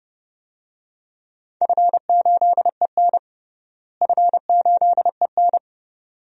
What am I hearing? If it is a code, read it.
Morse code, heard as a keyed tone, about 32 wpm, 700 Hz. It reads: F8EN F8EN
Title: DXCC-100 Call Signs - 4 Characters - Repeated 2x - 30wpm